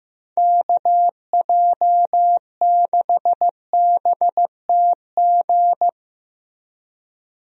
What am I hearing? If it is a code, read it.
Morse code, heard as a keyed tone, about 15 wpm, 700 Hz. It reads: KJ6BTG
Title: USA Call Signs - 6 Characters - Repeated 3x - 15wpm